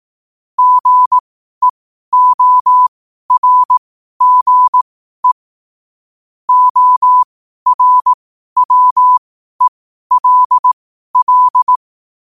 Morse code, heard as a keyed tone, about 18 wpm, spaced as for 12 wpm, 1000 Hz. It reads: GEORGE ORWELL
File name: qotdfiles / 12wpm